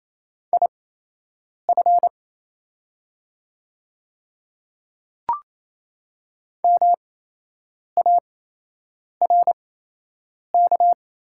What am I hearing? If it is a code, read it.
Morse code, heard as a keyed tone, about 28 wpm, spaced as for 8 wpm, 690 Hz. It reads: IF MARK
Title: 8x Character Spacing - Top 300 Words - 28wpm